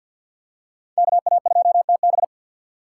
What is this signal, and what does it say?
Morse code, heard as a keyed tone, about 51 wpm, 700 Hz. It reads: KA2TB
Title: Code Groups - 5 Letter:Numbers - 50wpm